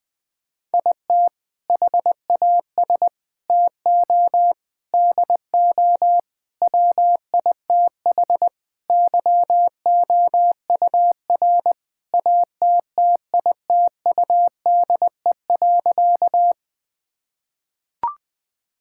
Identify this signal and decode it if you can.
Morse code, heard as a keyed tone, about 21 wpm, 700 Hz. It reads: IT HAS TO DO WITH YOUR ATTITUDE.
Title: Sentences with 5-Word Phrases - Spoken Beginning - 20wpm